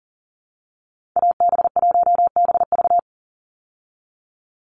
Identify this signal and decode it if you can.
Morse code, heard as a keyed tone, about 40 wpm, 690 Hz. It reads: AB1BV
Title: USA Call Signs 40wpm